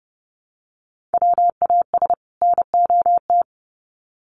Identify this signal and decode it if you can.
Morse code, heard as a keyed tone, about 30 wpm, 700 Hz. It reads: WAS NOT
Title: Sets of 2 Words 30wpm